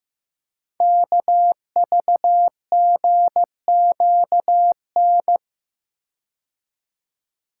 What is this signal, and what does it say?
Morse code, heard as a keyed tone, about 15 wpm, 700 Hz. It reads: KVGQN
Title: Code Groups - 5 Letters - 15wpm